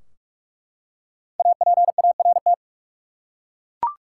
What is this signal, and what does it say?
Morse code, heard as a keyed tone, about 42 wpm, 700 Hz. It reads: APART
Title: Five Letter Words - 45wpm